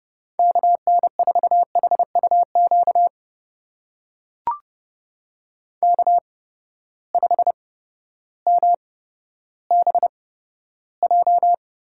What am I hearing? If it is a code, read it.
Morse code, heard as a keyed tone, about 30 wpm, spaced as for 14 wpm, 700 Hz. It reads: KN4HUQ K5MBJ
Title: USA Call Signs - 8x Character Spacing - 30wpm